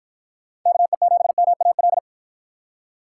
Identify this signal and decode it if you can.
Morse code, heard as a keyed tone, about 48 wpm, 700 Hz. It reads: KE7MAL